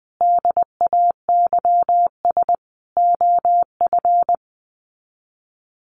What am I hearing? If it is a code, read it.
Morse code, heard as a keyed tone, about 20 wpm, 700 Hz. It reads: DAYS OF
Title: Sets of 5 Words Encore 20wpm